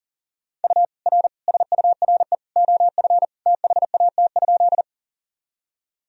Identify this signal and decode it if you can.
Morse code, heard as a keyed tone, about 40 wpm, 700 Hz. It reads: U R SURE OF THAT?
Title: Sentences from Top 300 Words and Abbreviations 40wpm